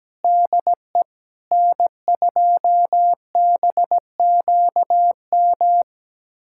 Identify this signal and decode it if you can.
Morse code, heard as a keyed tone, about 17 wpm, 700 Hz. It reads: DE N2BQM
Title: USA Callsign DE Callsign - 5 Characters - 17wpm